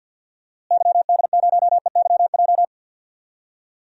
Morse code, heard as a keyed tone, about 52 wpm, 700 Hz. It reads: YD0EYJ
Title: DXCC-100 Call Signs - 6 Characters - Repeated 2x - 50wpm